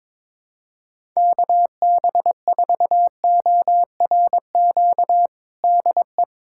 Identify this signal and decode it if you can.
Morse code, heard as a keyed tone, about 21 wpm, 700 Hz. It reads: KB4ORQ DE
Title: USA Callsign DE Callsign - 6 Characters - 22wpm